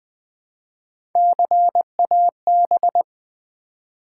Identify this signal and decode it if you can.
Morse code, heard as a keyed tone, about 20 wpm, 700 Hz. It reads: CAB
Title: Build A Word - 1 Letter At A Time - 5-Letter Words - 20wpm